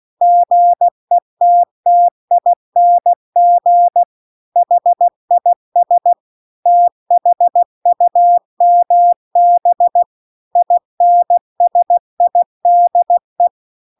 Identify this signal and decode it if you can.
Morse code, heard as a keyed tone, about 16 wpm, 700 Hz. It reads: GETTING HIS THUMB INSIDE
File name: CW-The-Big-Twoheaded-River-2-16WPM